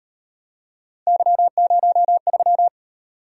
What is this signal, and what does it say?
Morse code, heard as a keyed tone, about 37 wpm, 700 Hz. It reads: Y03